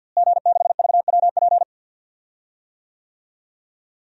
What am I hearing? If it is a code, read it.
Morse code, heard as a keyed tone, about 51 wpm, 700 Hz. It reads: G6VWP
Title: DXCC-100 Call Signs - 5 Characters - Repeated 2x - 50wpm